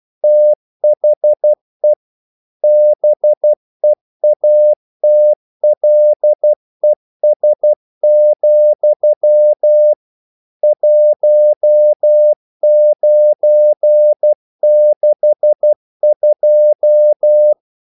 Morse code, heard as a keyed tone, about 12 wpm, 600 Hz. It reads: THE BEATLES, 1962
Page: English quotations in CW (telegraphy - morse)